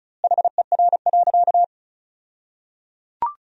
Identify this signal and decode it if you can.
Morse code, heard as a keyed tone, about 36 wpm, 700 Hz. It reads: HER.
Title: 2x Word-Spacing - Sentences from Top 200 Words - 35wpm